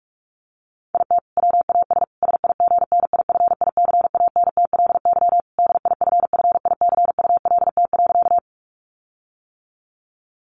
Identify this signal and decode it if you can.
Morse code, heard as a keyed tone, about 44 wpm, 700 Hz. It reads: IT WAS SIGNIFICANTLY DIFFICULT.
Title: Sentences from Top 1500 Words 45wpm